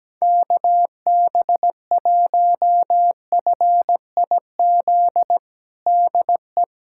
Morse code, heard as a keyed tone, about 17 wpm, 700 Hz. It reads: KB1FIZ DE
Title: USA Callsign DE Callsign 17wpm